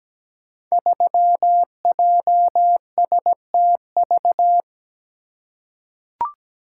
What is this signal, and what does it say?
Morse code, heard as a keyed tone, about 17 wpm, 700 Hz. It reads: 3JSTV
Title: Code Groups - 5 Letter:Numbers - 17wpm